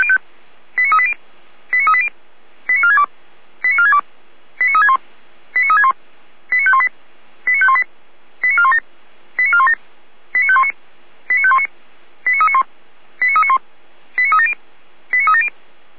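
Five-tone selective call (ZVEI), heard as quick runs of five tones, tones of about 70 ms. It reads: 89380 89380 89462 89462 89361 89361 89518 89518 89417 89417 89519 89519 89391 89391 89370 89370
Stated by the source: Received on SDR, ZVEI five-tone call, address 89120.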